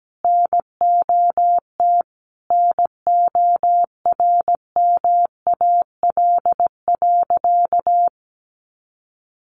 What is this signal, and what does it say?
Morse code, heard as a keyed tone, about 17 wpm, 700 Hz. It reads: NOT NORMAL.
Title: Sentences from Top 1200 Words 17wpm